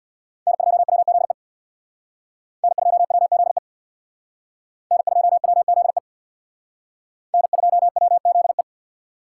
Morse code, heard as a keyed tone, about 60 wpm, 700 Hz. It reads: N2WZE N2WZE N2WZE N2WZE
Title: Speed Racing - USA 1x3 Call Signs 50wpm